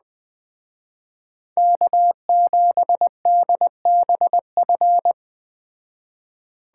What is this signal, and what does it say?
Morse code, heard as a keyed tone, about 21 wpm, 700 Hz. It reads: K7DBF